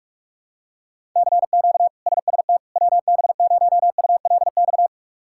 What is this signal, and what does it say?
Morse code, heard as a keyed tone, about 42 wpm, 700 Hz. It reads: CQ SST WB0ULX
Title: K1USN SST QSO Practice - 45wpm